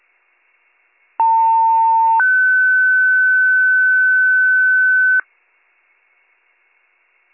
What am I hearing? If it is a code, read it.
Two-tone page, heard as a one-second tone, then a three-second tone, about 900 then 1500 Hz. It is heard from 1.2 to 5.2 s.